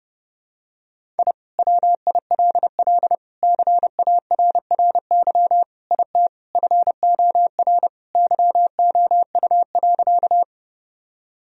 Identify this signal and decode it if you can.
Morse code, heard as a keyed tone, about 30 wpm, 700 Hz. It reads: I WILL CARRY IT FOR YOU.
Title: Sentences from Top 1000 Words - Review Part 1 - 30wpm